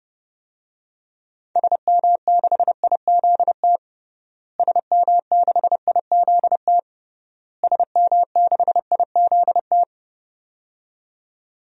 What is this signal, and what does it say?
Morse code, heard as a keyed tone, about 29 wpm, 700 Hz. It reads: SM6IZT SM6IZT SM6IZT
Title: DXCC-100 Call Signs - 6 Characters - Repeated 3x - 30wpm